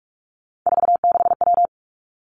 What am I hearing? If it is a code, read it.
Morse code, heard as a keyed tone, about 46 wpm, 700 Hz. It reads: 46W
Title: Three Challenging Letter-Number Combinations 45wpm